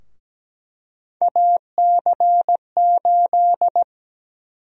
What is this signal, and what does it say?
Morse code, heard as a keyed tone, about 17 wpm, 700 Hz. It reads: AC8
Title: USA Call Sign Prefixes - 17wpm